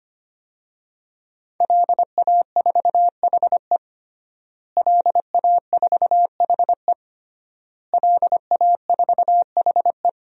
Morse code, heard as a keyed tone, about 25 wpm, 700 Hz. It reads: LA4HE LA4HE LA4HE
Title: DXCC-100 Call Signs - 5 Characters - Repeated 3x - 25wpm